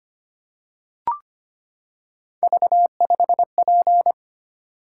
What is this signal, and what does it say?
Morse code, heard as a keyed tone, about 25 wpm, 700 Hz. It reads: V5P
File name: Three Letter-Number Combinations - 25wpm